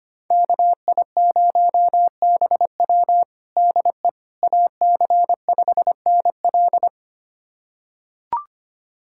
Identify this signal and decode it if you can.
Morse code, heard as a keyed tone, about 25 wpm, 700 Hz. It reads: KI0BW DE AC5NL
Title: USA Callsign DE Callsign - 5 Characters - 25wpm